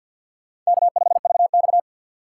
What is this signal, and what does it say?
Morse code, heard as a keyed tone, about 64 wpm, 700 Hz. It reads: K5VX